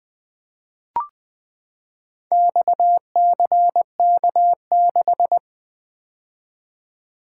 Morse code, heard as a keyed tone, about 20 wpm, 700 Hz. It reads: XCK6
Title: Four Letter-Number Combinations - 20wpm